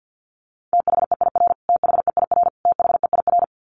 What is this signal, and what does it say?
Morse code, heard as a keyed tone, about 50 wpm, 700 Hz. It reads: THEIR THEIR THEIR
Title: Top 500 Words Repeated 50wpm